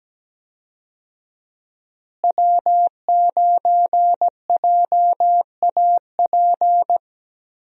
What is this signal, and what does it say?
Morse code, heard as a keyed tone, about 17 wpm, 700 Hz. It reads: W9JAP
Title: USA Call Signs - 5 Characters - Repeated 3x - 17wpm